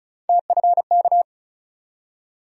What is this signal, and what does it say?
Morse code, heard as a keyed tone, about 36 wpm, 700 Hz. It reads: TFK
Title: USA Call Sign Postfixes - 3 Letters 35wpm